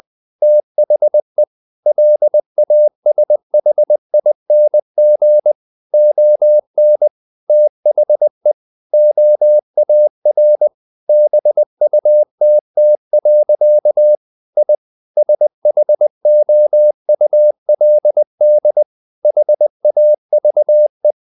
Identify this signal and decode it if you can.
Morse code, heard as a keyed tone, about 20 wpm, 600 Hz. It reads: THE LASHING ON THE OAR BUTT. I SHOULD HAVE
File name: the old man and the sea part 2 20wpm